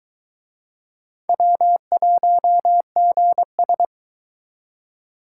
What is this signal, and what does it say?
Morse code, heard as a keyed tone, about 21 wpm, 700 Hz. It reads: W1GS